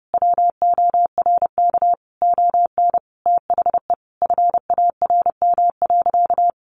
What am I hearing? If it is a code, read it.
Morse code, heard as a keyed tone, about 30 wpm, 700 Hz. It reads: WORK ON THE FARM.
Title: Sentences from Top 1200 Words 30wpm